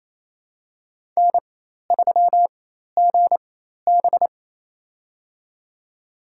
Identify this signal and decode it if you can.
Morse code, heard as a keyed tone, about 28 wpm, spaced as for 13 wpm, 700 Hz. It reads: N3GB
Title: USA Call Signs - 4 Characters - 4x Character Spacing - 28wpm